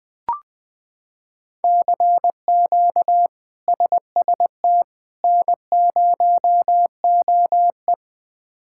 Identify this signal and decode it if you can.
Morse code, heard as a keyed tone, about 20 wpm, 700 Hz. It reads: CQ SST N0OE
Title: K1USN SST QSO Practice - 20wpm